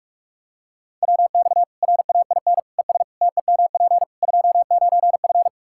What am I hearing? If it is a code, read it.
Morse code, heard as a keyed tone, about 46 wpm, 700 Hz. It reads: WX RAIN ES TEMP 29F
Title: Practice QSO - Exchange 2 - Info - 45wpm